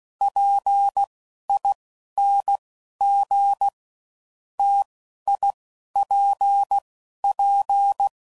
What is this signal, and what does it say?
Morse code, heard as a keyed tone, about 16 wpm, 800 Hz. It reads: P I N G T I P P
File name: Combinations / Difficult Combos / T AOEI's